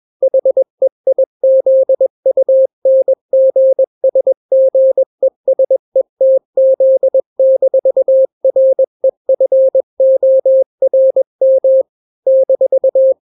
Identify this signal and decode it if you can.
Morse code, heard as a keyed tone, about 21 wpm, 530 Hz. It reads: HEIZUNGSGESETZ-REFORM -
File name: daily / 20wpm